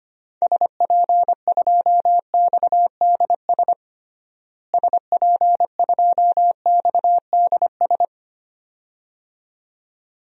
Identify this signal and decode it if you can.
Morse code, heard as a keyed tone, about 25 wpm, 700 Hz. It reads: SP2XDS SP2XDS
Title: DXCC-100 Call Signs - 6 Characters - Repeated 2x - 25wpm